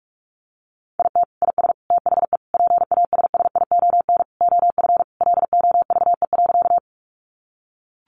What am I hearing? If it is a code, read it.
Morse code, heard as a keyed tone, about 44 wpm, 700 Hz. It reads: IT IS THE PASSION OF LOVE.
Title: Sentences from Top 1300 Words 45wpm